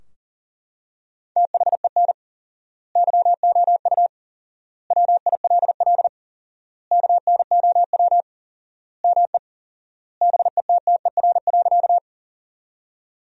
Morse code, heard as a keyed tone, about 40 wpm, 700 Hz. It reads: THEN YOU WILL KNOW ME BETTER.
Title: 4x Word-Spacing - Sentences from Top 400 Words - 40wpm